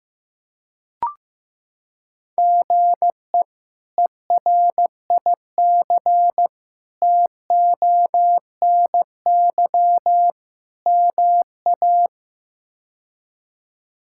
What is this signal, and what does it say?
Morse code, heard as a keyed tone, about 15 wpm, 700 Hz. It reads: GE ERIC TONY MA
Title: K1USN SST QSO Practice - 15wpm